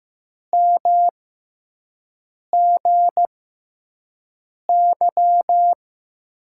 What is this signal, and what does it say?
Morse code, heard as a keyed tone, about 15 wpm, spaced as for 5 wpm, 700 Hz. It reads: MGY
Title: USA Call Signs - 5 Characters - 6x Character Spacing - 15wpm